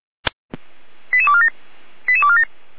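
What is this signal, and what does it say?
Five-tone selective call (ZVEI), heard as quick runs of five tones, tones of about 70 ms. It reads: 88247 88247
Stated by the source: ZVEI five-tone selective call received on SDR.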